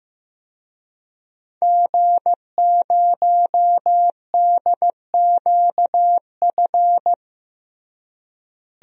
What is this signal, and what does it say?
Morse code, heard as a keyed tone, about 15 wpm, 700 Hz. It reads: G0DQF
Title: DXCC-100 Call Signs - 5 Characters - Repeated 3x - 15wpm